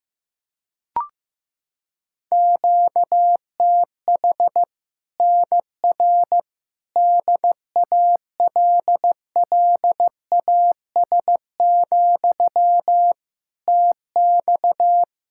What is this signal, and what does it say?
Morse code, heard as a keyed tone, about 15 wpm, 700 Hz. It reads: QTH NR DALLAS, TX
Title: Practice QSO - Exchange 1 - Beginning - 15wpm